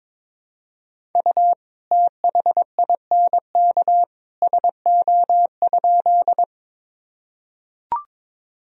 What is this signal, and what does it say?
Morse code, heard as a keyed tone, about 22 wpm, 700 Hz. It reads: U THINK SO?
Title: Sentences from Top 200 Words and Abbreviations 22wpm